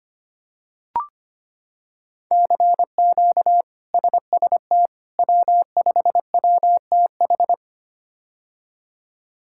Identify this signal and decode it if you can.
Morse code, heard as a keyed tone, about 25 wpm, 700 Hz. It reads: CQ SST W5WTH
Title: K1USN SST QSO Practice - 25wpm